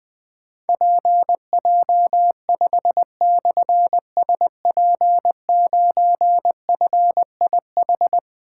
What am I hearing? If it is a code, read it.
Morse code, heard as a keyed tone, about 20 wpm, 700 Hz. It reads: PJ5/SP9FIH
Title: Difficult Call Signs 20wpm